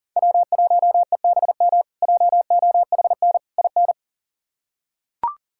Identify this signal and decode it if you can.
Morse code, heard as a keyed tone, about 40 wpm, 700 Hz. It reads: W1EBM JOHN IN
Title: K1USN SST QSO Practice - 40wpm